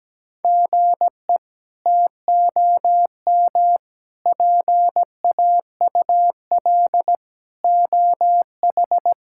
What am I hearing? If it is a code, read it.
Morse code, heard as a keyed tone, about 17 wpm, 700 Hz. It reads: GE TOM PAUL OH
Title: K1USN SST QSO Practice - 17wpm